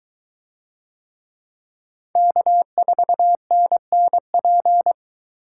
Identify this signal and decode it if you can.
Morse code, heard as a keyed tone, about 22 wpm, 700 Hz. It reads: K4NNP